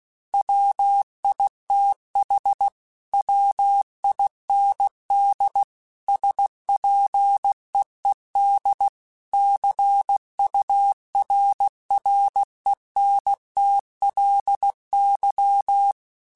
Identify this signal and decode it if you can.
Morse code, heard as a keyed tone, about 16 wpm, 800 Hz. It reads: WITH WIND SPEED CURRENTLY
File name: Weather Sets / Weather Sets-mp3s / Weather 5 12 @15